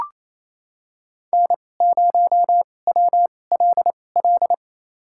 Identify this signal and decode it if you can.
Morse code, heard as a keyed tone, about 27 wpm, spaced as for 20 wpm, 700 Hz. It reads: N0WLL
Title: USA Call Signs - 2x Character Spacing - 28wpm